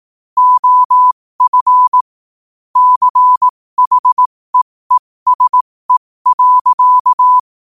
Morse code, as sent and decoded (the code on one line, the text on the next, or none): --- ..-. / -.-. .... . ... .-.-.-
OF CHEESE.